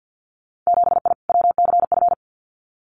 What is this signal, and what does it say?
Morse code, heard as a keyed tone, about 50 wpm, 700 Hz. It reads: 7I WCF
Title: Winter Field Day Exchanges Encore 50wpm